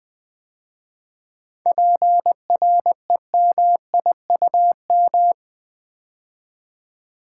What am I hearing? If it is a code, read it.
Morse code, heard as a keyed tone, about 20 wpm, 700 Hz. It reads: PREMIUM
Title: 7 Letter Words - First 3 Letters Given - 20wpm